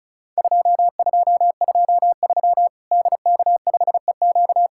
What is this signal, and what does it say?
Morse code, heard as a keyed tone, about 36 wpm, 700 Hz. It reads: 2223 DK5EQ
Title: Worked All Europe DX Contest - QTC Exchange - 35wpm